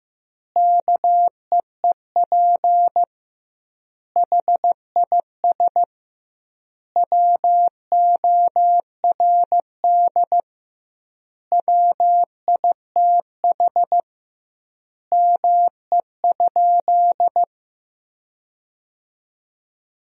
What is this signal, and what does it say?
Morse code, heard as a keyed tone, about 15 wpm, 700 Hz. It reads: KEEP HIS WORD WITH ME?
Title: 2x Word-Spacing - Sentences from Top 300 Words - 15wpm